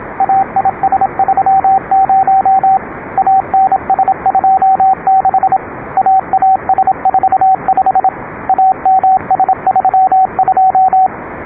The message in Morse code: AIS30 ANS26 AAS45 AMS32